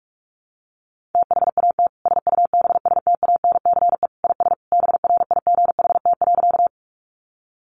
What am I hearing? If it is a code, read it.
Morse code, heard as a keyed tone, about 44 wpm, 700 Hz. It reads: THAT SUBSTANCE IS BRIGHT.
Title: Sentences from Top 1500 Words 45wpm